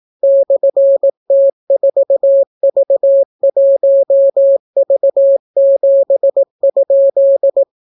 Morse code, heard as a keyed tone, about 18 wpm, 550 Hz. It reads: /T4V1V7?